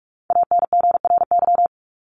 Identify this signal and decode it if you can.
Morse code, heard as a keyed tone, about 44 wpm, 700 Hz. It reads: ANGRY
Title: Top 1751-2000 Words 45wpm